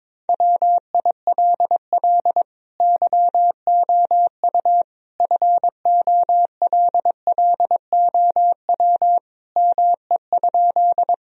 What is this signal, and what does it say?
Morse code, heard as a keyed tone, about 22 wpm, 700 Hz. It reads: WILL YOU FOLLOW ME?
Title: Sentences from Top 1000 Words - Review Part 1 - 22wpm